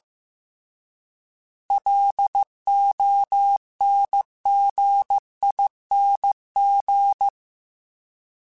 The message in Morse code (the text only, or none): LONGING